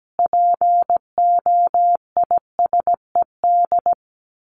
Morse code, heard as a keyed tone, about 17 wpm, 700 Hz. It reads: POISED
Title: A Sound Of Thunder 17wpm